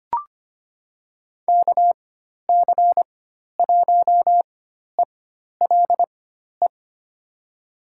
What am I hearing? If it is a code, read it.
Morse code, heard as a keyed tone, about 25 wpm, spaced as for 12 wpm, 700 Hz. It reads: KC1ELE
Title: USA Call Signs - 6 Characters - 4x Character Spacing - 25wpm